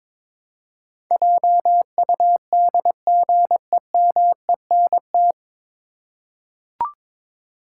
Morse code, heard as a keyed tone, about 22 wpm, 700 Hz. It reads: JUDGEMENT
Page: Morse Code Ninja practice 22wpm